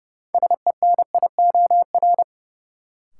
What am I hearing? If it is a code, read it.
Morse code, heard as a keyed tone, about 30 wpm, 700 Hz. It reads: SENIOR
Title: Sets of 5 Words Encore - New Word Drill - 30wpm